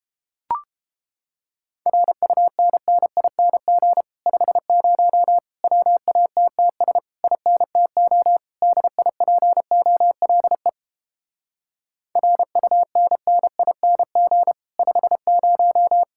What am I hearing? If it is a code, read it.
Morse code, heard as a keyed tone, about 31 wpm, 700 Hz. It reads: RUNNING 50 WATTS INTO DIPOLE RUNNING 50